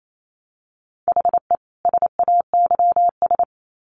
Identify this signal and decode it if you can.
Morse code, heard as a keyed tone, about 28 wpm, 700 Hz. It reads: HE SAYS
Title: Sets of 2 Words 28wpm